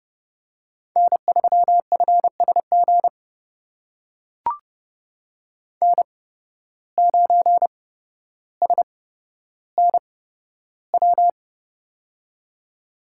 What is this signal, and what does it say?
Morse code, heard as a keyed tone, about 30 wpm, spaced as for 13 wpm, 700 Hz. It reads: N3FSG N9SNW
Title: USA Call Signs - 5 Characters - 8x Character Spacing - 30wpm